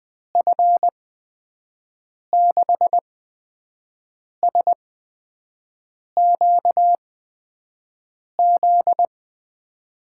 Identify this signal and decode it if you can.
Morse code, heard as a keyed tone, about 21 wpm, spaced as for 5 wpm, 700 Hz. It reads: F6SQZ
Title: USA Call Signs - 6 Characters - 8x Character Spacing - 20wpm